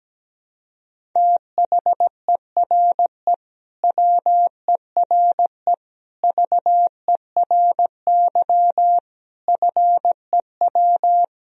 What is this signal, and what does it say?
Morse code, heard as a keyed tone, about 17 wpm, 700 Hz. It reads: THERE WERE VERY FEW
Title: Sentences from Top 1000 Words - Review Part 2 - 17wpm